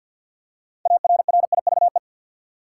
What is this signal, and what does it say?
Morse code, heard as a keyed tone, about 51 wpm, 700 Hz. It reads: ARRIVE
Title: Six Letter Words - 50wpm